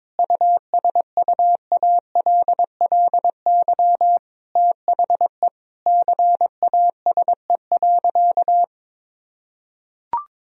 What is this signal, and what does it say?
Morse code, heard as a keyed tone, about 22 wpm, 700 Hz. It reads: USUALLY THE CASE.
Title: Sentences from Top 1000 Words - Review Part 3 - 22wpm